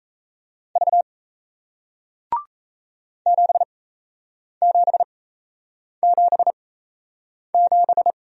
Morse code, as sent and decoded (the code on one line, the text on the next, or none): ...- / --... --... --... --...
V 7777